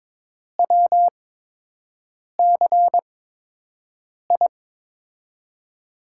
Morse code, heard as a keyed tone, about 22 wpm, spaced as for 6 wpm, 700 Hz. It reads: WCI
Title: USA Call Signs - 8x Character Spacing - 22wpm